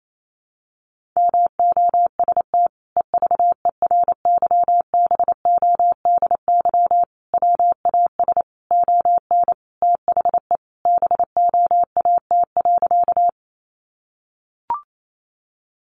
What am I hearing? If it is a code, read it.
Morse code, heard as a keyed tone, about 28 wpm, 700 Hz. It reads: MOST EVERYBODY WAS ON THE BOAT.
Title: Sentences from Top 1400 Words 28wpm